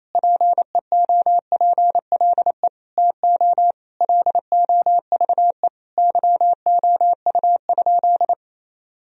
Morse code, heard as a keyed tone, about 28 wpm, 700 Hz. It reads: PEOPLE TO LOVE YOU?
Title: Sentences from Top 1000 Words - Review Part 3 - 28wpm